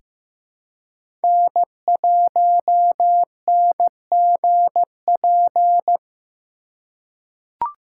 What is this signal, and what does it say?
Morse code, heard as a keyed tone, about 16 wpm, 700 Hz. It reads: N1NGP
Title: USA Call Signs - 2x Character Spacing - 15wpm